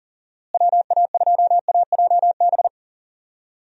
Morse code, heard as a keyed tone, about 38 wpm, 700 Hz. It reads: WA2AJB
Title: USA Call Signs - 6 Characters - Repeated 3x - 40wpm